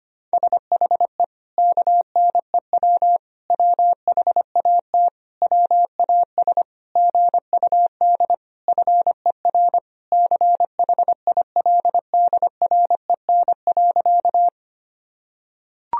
Morse code, heard as a keyed tone, about 25 wpm, 700 Hz. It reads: SHE KNEW WHAT WAS GUD FER CHILDREN.
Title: Sentences from Top 400 Words and Abbreviations 25wpm